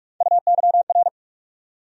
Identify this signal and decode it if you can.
Morse code, heard as a keyed tone, about 46 wpm, 700 Hz. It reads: UYR
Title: USA Call Sign Postfixes - 3 Letters 45wpm